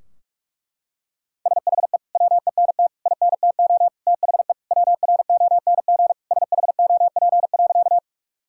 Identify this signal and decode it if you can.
Morse code, heard as a keyed tone, about 41 wpm, 700 Hz. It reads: SHE WENT INTO THE WRONG SHOP.
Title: Sentences from Top 1600 Words 45wpm